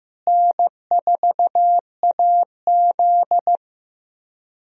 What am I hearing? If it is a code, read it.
Morse code, heard as a keyed tone, about 15 wpm, 700 Hz. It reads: N4AZ
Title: USA Call Signs - 4 Characters - Repeated 2x - 15wpm